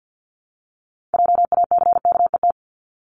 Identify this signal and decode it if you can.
Morse code, heard as a keyed tone, about 50 wpm, 700 Hz. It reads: JACKET
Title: Top 1751-2000 Words 50wpm